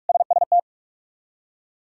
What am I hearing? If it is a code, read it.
Morse code, heard as a keyed tone, about 46 wpm, 700 Hz. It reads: SST
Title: K1USN SST QSO Practice - 45wpm